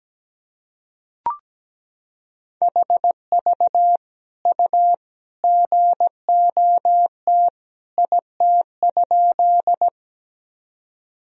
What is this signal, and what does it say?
Morse code, heard as a keyed tone, about 17 wpm, 700 Hz. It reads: HV U GOT IT?
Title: Sentences from Top 300 Words and Abbreviations 17wpm